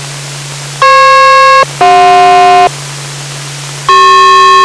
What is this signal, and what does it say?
Aircraft SELCAL, heard as two pairs of tones. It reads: FP-CJ